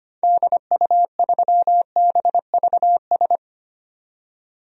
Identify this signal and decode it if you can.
Morse code, heard as a keyed tone, about 25 wpm, 700 Hz. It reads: DU3BVS
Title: DXCC-100 Call Signs - 6 Characters - 25wpm